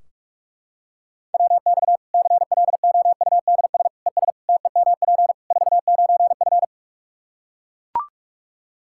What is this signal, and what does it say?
Morse code, heard as a keyed tone, about 41 wpm, 700 Hz. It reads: WX CLOUDS ES TEMP 49F
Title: K1USN SST QSO Practice 45wpm